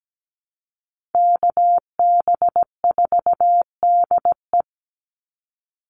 Morse code, heard as a keyed tone, about 17 wpm, 700 Hz. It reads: KB4DE
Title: Call Signs - 17wpm